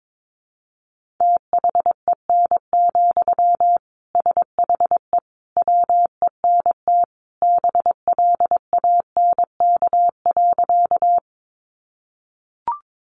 Morse code, heard as a keyed tone, about 22 wpm, 700 Hz. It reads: THEN, SHE WENT BLANK.
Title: Sentences from Top 1500 Words 22wpm